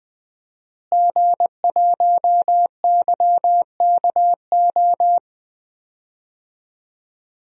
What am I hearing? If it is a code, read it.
Morse code, heard as a keyed tone, about 20 wpm, 700 Hz. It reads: G1YKO
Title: DXCC-100 Call Signs - 5 Characters - Repeated 3x - 20wpm